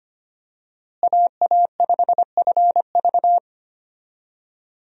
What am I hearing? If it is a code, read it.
Morse code, heard as a keyed tone, about 25 wpm, 700 Hz. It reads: AA5FV
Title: USA Call Signs - 5 Characters - Repeated 2x - 25wpm